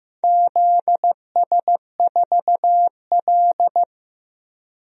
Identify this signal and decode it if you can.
Morse code, heard as a keyed tone, about 15 wpm, 700 Hz. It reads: ZS4L
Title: DXCC-100 Call Signs - 4 Characters - Repeated 2x - 15wpm